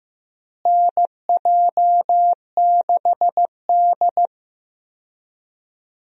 Morse code, heard as a keyed tone, about 15 wpm, 700 Hz. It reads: NJ6D
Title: USA Call Signs - 4 Characters - Repeated 2x - 15wpm